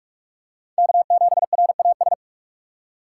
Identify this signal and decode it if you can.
Morse code, heard as a keyed tone, about 44 wpm, 700 Hz. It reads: K7RAS
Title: USA Call Signs - 5 Characters - 4x Character Spacing - 45wpm